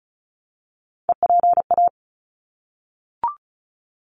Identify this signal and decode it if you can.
Morse code, heard as a keyed tone, about 34 wpm, 700 Hz. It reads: EPA
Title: ARRL Sections 35wpm